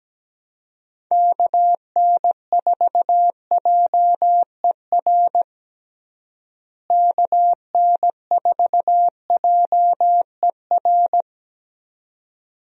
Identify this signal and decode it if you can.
Morse code, heard as a keyed tone, about 17 wpm, 700 Hz. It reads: KN4JER KN4JER
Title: USA Call Signs - 6 Characters - Repeated 2x - 17wpm